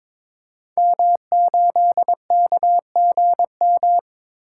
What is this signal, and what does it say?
Morse code, heard as a keyed tone, about 22 wpm, 700 Hz. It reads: M8KGM
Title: Code Groups - 5 Letter:Numbers - 22wpm